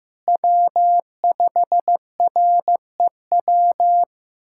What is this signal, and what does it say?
Morse code, heard as a keyed tone, about 15 wpm, 700 Hz. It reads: W5REW
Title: USA Callsign DE Callsign - 5 Characters - 15wpm